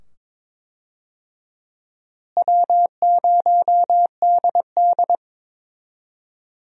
Morse code, heard as a keyed tone, about 21 wpm, 700 Hz. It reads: W0DD
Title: USA Call Signs - 4 Characters - Repeated 2x - 22wpm